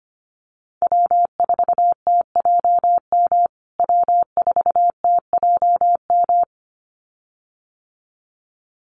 Morse code, heard as a keyed tone, about 25 wpm, 700 Hz. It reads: W4TJM W4TJM
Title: USA Call Signs Repeated 2x 25wpm